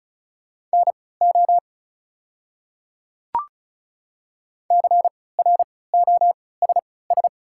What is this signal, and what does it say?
Morse code, heard as a keyed tone, about 36 wpm, spaced as for 20 wpm, 700 Hz. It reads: NO CROSS
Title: 3x Character Spacing - Top 300 Words - 35wpm